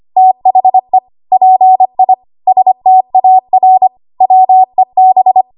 Morse code, heard as a keyed tone, about 25 wpm, 750 Hz. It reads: THE PI STAR WEB
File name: Archive / 25 WPM / 25WPM